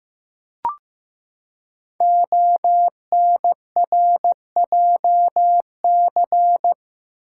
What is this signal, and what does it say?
Morse code, heard as a keyed tone, about 15 wpm, 700 Hz. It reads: ONRJC
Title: Code Groups - 5 Letters - 15wpm